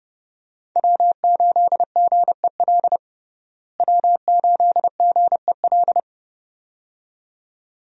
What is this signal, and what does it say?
Morse code, heard as a keyed tone, about 31 wpm, 700 Hz. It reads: W8GEL W8GEL
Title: USA Call Signs - 5 Characters - Repeated 2x - 30wpm